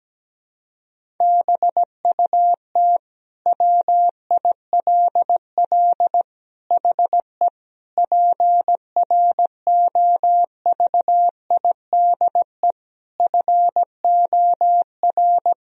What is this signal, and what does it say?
Morse code, heard as a keyed tone, about 17 wpm, 700 Hz. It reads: BUT WILL HE PROVIDE FOR
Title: Sentences from Top 1000 Words - Review Part 3 - 17wpm